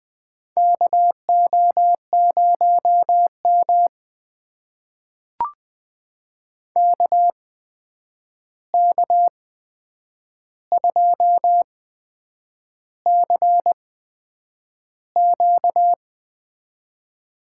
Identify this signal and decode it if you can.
Morse code, heard as a keyed tone, about 20 wpm, spaced as for 8 wpm, 700 Hz. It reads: KO0M KK2CQ
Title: USA Call Signs - 8x Character Spacing - 20wpm